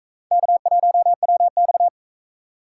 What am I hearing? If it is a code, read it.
Morse code, heard as a keyed tone, about 44 wpm, 700 Hz. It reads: K1WX